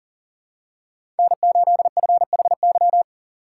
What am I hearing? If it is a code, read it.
Morse code, heard as a keyed tone, about 40 wpm, 700 Hz. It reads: N8FHY
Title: USA Call Signs - 5 Characters - 4x Character Spacing - 40wpm